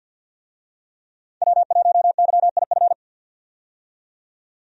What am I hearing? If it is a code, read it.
Morse code, heard as a keyed tone, about 52 wpm, 700 Hz. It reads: W1YIF
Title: USA Call Signs - 5 Characters - 50wpm